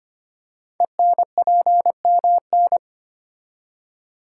Morse code, heard as a keyed tone, about 25 wpm, 700 Hz. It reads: ENPMN
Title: Code Groups - 5 Letter:Numbers - 25wpm